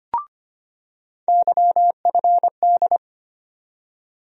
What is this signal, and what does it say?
Morse code, heard as a keyed tone, about 25 wpm, 700 Hz. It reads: YFD
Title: USA Call Sign Postfixes - 3 Letters 25wpm